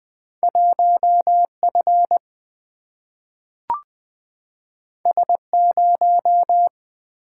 Morse code, heard as a keyed tone, about 21 wpm, 700 Hz. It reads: 1 F S 0
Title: Single-Letter Single-Number 20wpm